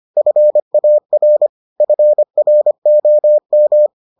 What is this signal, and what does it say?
Morse code, heard as a keyed tone, about 25 wpm, 600 Hz. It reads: FAR FROM